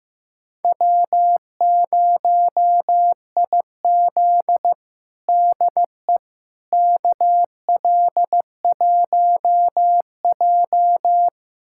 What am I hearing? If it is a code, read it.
Morse code, heard as a keyed tone, about 15 wpm, 700 Hz. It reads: W0IZ DE KL1J
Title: USA Callsign DE Callsign - 4 Characters - 15wpm